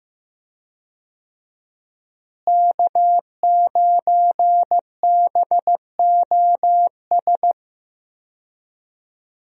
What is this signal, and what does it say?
Morse code, heard as a keyed tone, about 15 wpm, 700 Hz. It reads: K9BOS
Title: USA Call Signs - 5 Characters - Repeated 3x - 15wpm